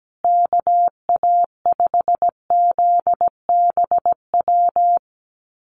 Morse code, heard as a keyed tone, about 17 wpm, 700 Hz. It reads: KA5ZBW
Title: USA Call Signs Repeated 2x 17wpm